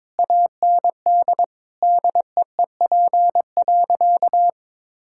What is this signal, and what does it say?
Morse code, heard as a keyed tone, about 21 wpm, 700 Hz. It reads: AND DEEP.
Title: Sentences from Top 1600 Words 22wpm